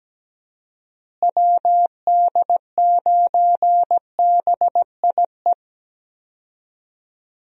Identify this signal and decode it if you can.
Morse code, heard as a keyed tone, about 17 wpm, 700 Hz. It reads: WD9BIE
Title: USA Call Signs - 6 Characters - Repeated 3x - 17wpm